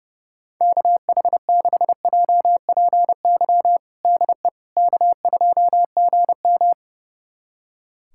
Morse code, heard as a keyed tone, about 30 wpm, 700 Hz. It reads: KH6JPY DE K2GM
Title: USA Callsign DE Callsign 30wpm